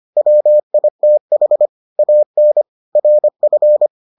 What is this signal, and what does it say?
Morse code, heard as a keyed tone, about 25 wpm, 600 Hz. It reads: WITH AN RF